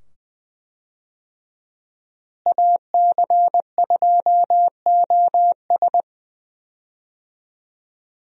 Morse code, heard as a keyed tone, about 20 wpm, 700 Hz. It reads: AC2OS
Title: USA Call Signs - 5 Characters - Repeated 2x - 20wpm